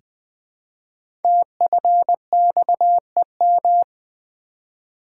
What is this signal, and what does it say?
Morse code, heard as a keyed tone, about 20 wpm, 700 Hz. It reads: TFXEM
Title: Code Groups - 5 Letters - 20wpm